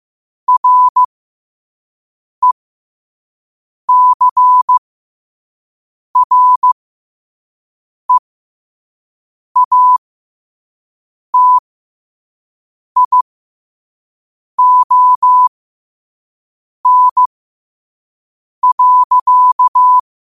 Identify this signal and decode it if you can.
Morse code, heard as a keyed tone, about 15 wpm, spaced as for 5 wpm, 1000 Hz. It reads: RECREATION.
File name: qotdfiles / 5wpm